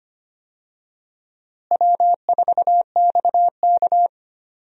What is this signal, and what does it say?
Morse code, heard as a keyed tone, about 25 wpm, 700 Hz. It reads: W4XK
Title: USA Call Signs - 4 Characters - Repeated 3x - 25wpm